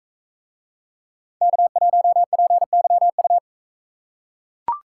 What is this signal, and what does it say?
Morse code, heard as a keyed tone, about 37 wpm, 700 Hz. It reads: K1PYU